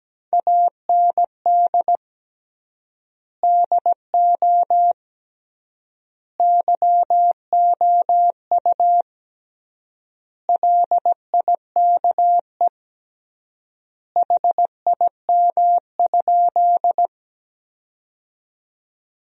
Morse code, heard as a keyed tone, about 17 wpm, 700 Hz. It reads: AND DO YOU LIKE HIM?
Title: 3x Word-Spacing - Sentences from Top 100 Words - 17wpm